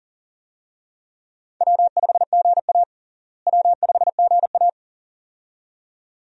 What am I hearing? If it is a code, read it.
Morse code, heard as a keyed tone, about 40 wpm, 700 Hz. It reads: W5GA W5GA
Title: USA Call Signs - 4 Characters - Repeated 2x - 40wpm